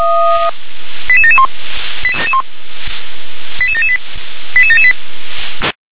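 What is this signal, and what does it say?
Five-tone selective call (ZVEI), heard as quick runs of five tones, tones of about 70 ms. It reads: 88771 88778 88778
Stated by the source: ZVEI five-tone selective call, code 88988, received with an SDR.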